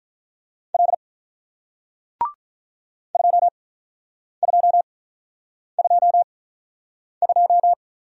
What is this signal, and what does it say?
Morse code, heard as a keyed tone, about 48 wpm, spaced as for 9 wpm, 700 Hz. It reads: L 2222